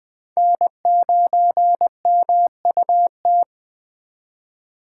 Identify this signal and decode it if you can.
Morse code, heard as a keyed tone, about 20 wpm, 700 Hz. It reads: N9MUT
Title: Speed Racing - USA 1x3 Call Signs 17wpm